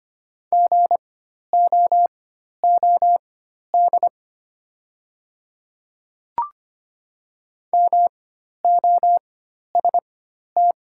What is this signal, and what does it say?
Morse code, heard as a keyed tone, about 25 wpm, spaced as for 12 wpm, 690 Hz. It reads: GOOD MOST
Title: 4x Character Spacing - Top 200 Words - 25wpm